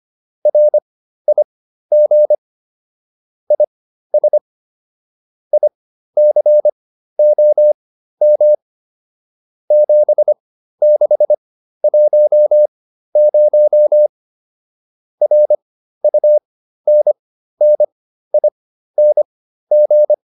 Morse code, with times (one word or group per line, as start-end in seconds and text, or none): 0.5-2.3 RIG
3.5-4.4 IS
5.5-8.5 ICOM
9.7-14.1 7610
15.2-20.1 RUNNING